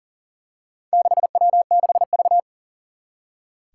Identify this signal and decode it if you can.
Morse code, heard as a keyed tone, about 42 wpm, 700 Hz. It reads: 6W6V
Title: Four Challenging Letter-Number Combinations 40wpm